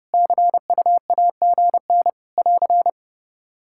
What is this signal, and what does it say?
Morse code, heard as a keyed tone, about 30 wpm, 700 Hz. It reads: CUAGN <AR>
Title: K1USN SST QSO Practice 30wpm